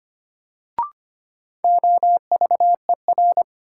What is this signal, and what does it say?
Morse code, heard as a keyed tone, about 25 wpm, 700 Hz. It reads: OVER
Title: Sentences from Top 100 Words - New Word Drill - 25wpm